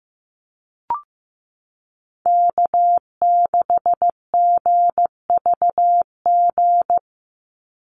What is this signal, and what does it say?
Morse code, heard as a keyed tone, about 15 wpm, 700 Hz. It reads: K6GVG
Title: Call Signs Repeated 3x - 15wpm